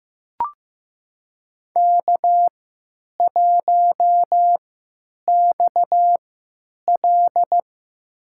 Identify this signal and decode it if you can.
Morse code, heard as a keyed tone, about 15 wpm, spaced as for 9 wpm, 700 Hz. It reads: K1XL
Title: USA Call Signs - 4 Characters - 3x Character Spacing - 15wpm